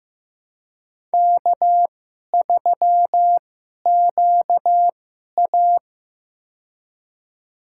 Morse code, heard as a keyed tone, about 15 wpm, spaced as for 11 wpm, 700 Hz. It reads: K3QA
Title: USA Call Signs - 4 Characters - 2x Character Spacing - 15wpm